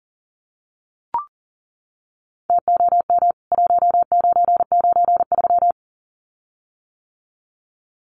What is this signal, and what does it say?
Morse code, heard as a keyed tone, about 39 wpm, 700 Hz. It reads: TOM 1993
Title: CWT Exchange 40wpm